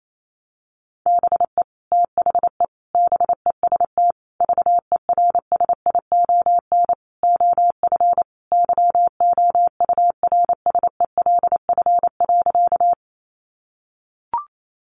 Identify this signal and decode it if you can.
Morse code, heard as a keyed tone, about 28 wpm, 700 Hz. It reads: BE THE BEST VERSION OF YOURSELF.
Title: Sentences from Top 1400 Words 28wpm